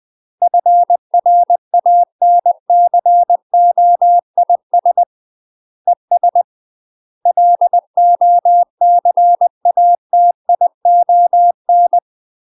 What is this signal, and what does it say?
Morse code, heard as a keyed tone, about 20 wpm, 700 Hz. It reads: FRANCOIS ES LOCATION
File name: UK MASTER QSOs TRAINING / 20WPM / UK MASTER QSOs 0032